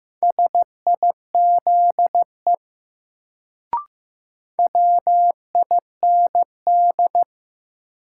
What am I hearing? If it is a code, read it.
Morse code, heard as a keyed tone, about 15 wpm, 700 Hz. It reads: SIZE WIND
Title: Sentences from Top 400 Words - New Word Drill - 15wpm